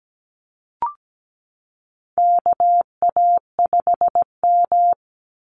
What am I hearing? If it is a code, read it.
Morse code, heard as a keyed tone, about 17 wpm, 700 Hz. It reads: KA5M
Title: Call Signs Repeated 2x - 17wpm